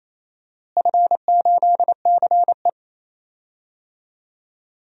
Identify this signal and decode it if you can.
Morse code, heard as a keyed tone, about 27 wpm, 700 Hz. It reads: F8CE
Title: DXCC-100 Call Signs - 4 Characters - Repeated 3x - 28wpm